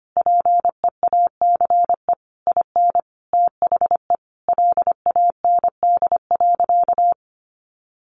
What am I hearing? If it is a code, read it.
Morse code, heard as a keyed tone, about 25 wpm, 700 Hz. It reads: PEACE IN THE LAND.
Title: Sentences from Top 1500 Words 25wpm